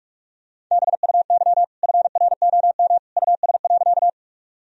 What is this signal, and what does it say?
Morse code, heard as a keyed tone, about 46 wpm, 700 Hz. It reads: BUY FROM US.
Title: Sentences from Top 1000 Words - Review Part 4 - 45wpm